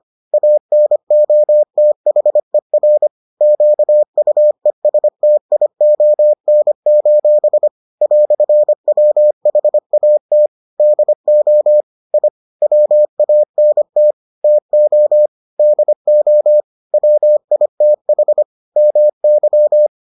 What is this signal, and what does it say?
Morse code, heard as a keyed tone, about 25 wpm, 600 Hz. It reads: ANOTHER QUESTION: "WHAT DO I WANT TO DO WITH MY